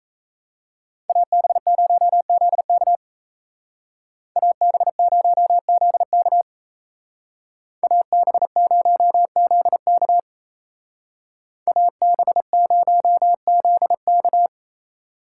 Morse code, heard as a keyed tone, about 34 wpm, 700 Hz. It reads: AB0ZK AB0ZK AB0ZK AB0ZK
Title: Speed Racing - USA 2x2 Call Signs 28wpm